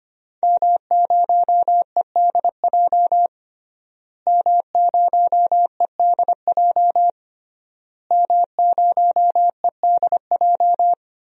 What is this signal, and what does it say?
Morse code, heard as a keyed tone, about 25 wpm, 700 Hz. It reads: M0EDJ M0EDJ M0EDJ
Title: DXCC-100 Call Signs - 5 Characters - Repeated 3x - 25wpm